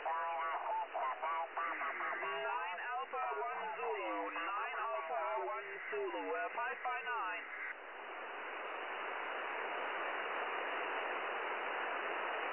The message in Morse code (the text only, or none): none